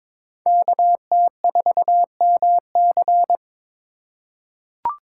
Morse code, heard as a keyed tone, about 21 wpm, 700 Hz. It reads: KT4MC
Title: USA Call Signs - 5 Characters - 3x Character Spacing - 22wpm